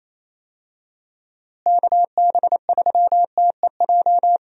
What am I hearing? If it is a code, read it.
Morse code, heard as a keyed tone, about 28 wpm, 700 Hz. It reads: KB3TEJ
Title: USA Call Signs - 6 Characters - Repeated 2x - 28wpm